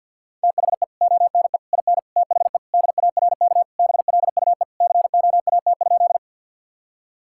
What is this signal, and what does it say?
Morse code, heard as a keyed tone, about 50 wpm, 700 Hz. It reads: THE ONE IN THE DARK BLUE COAT?
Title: Sentences from Top 1600 Words 50wpm